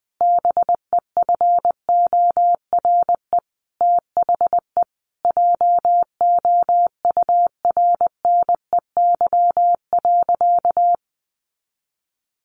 Morse code, heard as a keyed tone, about 20 wpm, 700 Hz. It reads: BEFORE THE JOURNEY.
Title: A Sound Of Thunder 20wpm